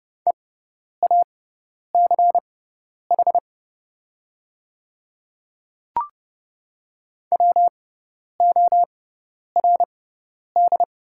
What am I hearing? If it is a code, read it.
Morse code, heard as a keyed tone, about 30 wpm, spaced as for 10 wpm, 690 Hz. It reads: EACH WORD
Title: 6x Character Spacing - Top 200 Words - 30wpm